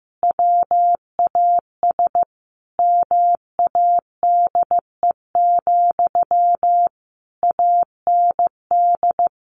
Morse code, heard as a keyed tone, about 14 wpm, 700 Hz. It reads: WAS MADE, AND
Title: Sentences from Top 1400 Words 15wpm